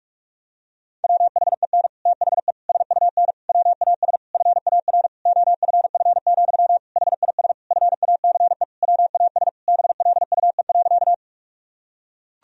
Morse code, heard as a keyed tone, about 43 wpm, 700 Hz. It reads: WHEN THE SUN WAS FAR OFF, HIS FACE WAS BLUE.
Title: Sentences from Top 1000 Words - Review Part 3 - 45wpm